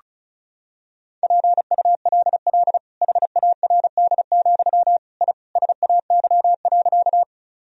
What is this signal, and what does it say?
Morse code, heard as a keyed tone, about 36 wpm, 700 Hz. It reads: PULL HARD, I SAY.
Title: Sentences from Top 1000 Words - Review Part 2 - 35wpm